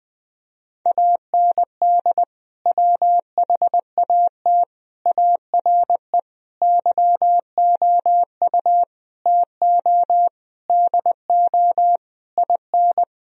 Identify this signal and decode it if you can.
Morse code, heard as a keyed tone, about 21 wpm, 700 Hz. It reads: AND WHAT ARE YOU TO DO IN
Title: Sentences from Top 1000 Words - Review Part 2 - 20wpm